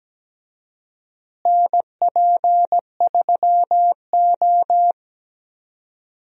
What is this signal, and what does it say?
Morse code, heard as a keyed tone, about 17 wpm, 700 Hz. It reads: NP3O
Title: USA Call Signs - 4 Characters - Repeated 2x - 17wpm